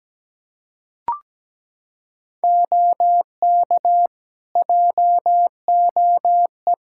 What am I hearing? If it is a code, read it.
Morse code, heard as a keyed tone, about 17 wpm, 700 Hz. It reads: OK JOE
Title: K1USN SST QSO Practice 17wpm